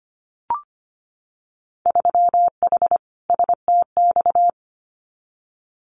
Morse code, heard as a keyed tone, about 25 wpm, 700 Hz. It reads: 3H STX
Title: Winter Field Day Exchanges 25wpm